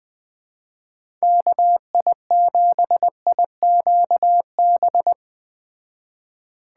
Morse code, heard as a keyed tone, about 21 wpm, 700 Hz. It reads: KI7IQB
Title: Speed Racing - USA 2x3 Call Signs 17wpm